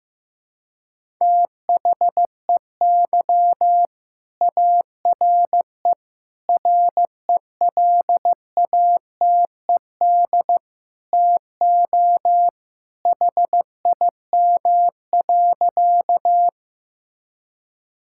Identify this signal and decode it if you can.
Morse code, heard as a keyed tone, about 15 wpm, 700 Hz. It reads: THEY ARE RELATED TO HIM.
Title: Sentences from Top 1000 Words - Review Part 5 - 15wpm